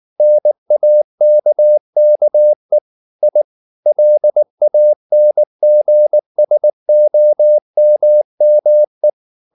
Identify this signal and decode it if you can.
Morse code, heard as a keyed tone, about 19 wpm, 600 Hz. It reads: NAKKE I LANGSOMME
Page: Ferie 19 wpm